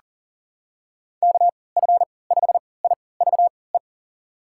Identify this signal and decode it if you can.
Morse code, heard as a keyed tone, about 40 wpm, spaced as for 23 wpm, 700 Hz. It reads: KF5IVE
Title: USA Call Signs - 3x Character Spacing - 40wpm